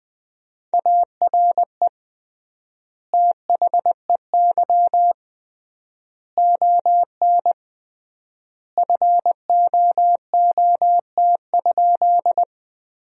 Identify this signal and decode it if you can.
Morse code, heard as a keyed tone, about 21 wpm, 700 Hz. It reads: ARE THEY ON FOOT?
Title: 3x Word-Spacing - Sentences from Top 500 Words - 20wpm